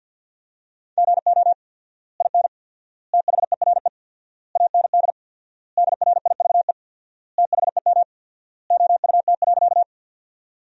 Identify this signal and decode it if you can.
Morse code, heard as a keyed tone, about 53 wpm, 700 Hz. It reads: GO IN THERE AND DRIVE THEM OUT.
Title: 4x Word-Spacing - Sentences from Top 500 Words - 50wpm